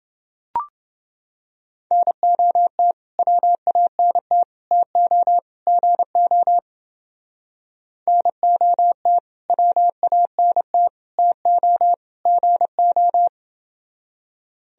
Morse code, heard as a keyed tone, about 30 wpm, 700 Hz. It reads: NOT WANT TO GO NOT WANT TO GO